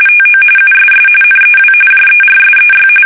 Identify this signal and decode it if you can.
RTTY, about 45.45 baud. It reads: TO USS NEW JERSEY
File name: fsk850 / battleshipnj